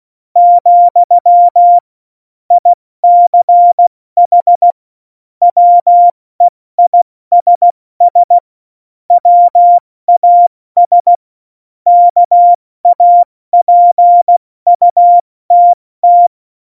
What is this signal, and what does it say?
Morse code, heard as a keyed tone, about 16 wpm, 700 Hz. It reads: , ICH WEISS WAS KAPUTT
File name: Morsekurs / Witze / Witz 29. Januar 2019, 70 F80